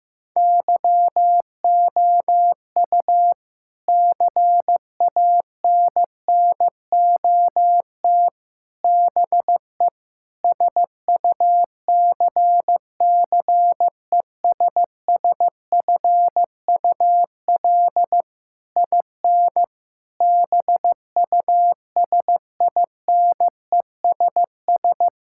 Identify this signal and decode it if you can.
Morse code, heard as a keyed tone, about 15 wpm, 700 Hz. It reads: YOU CANNOT BE SUCCESSFUL IN BUSINESS
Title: Sentences from Top 1000 Words - Review Part 6 - 15wpm